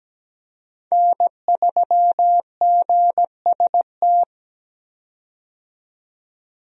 Morse code, heard as a keyed tone, about 17 wpm, 700 Hz. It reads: N3GST
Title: USA Call Signs - 5 Characters - 17wpm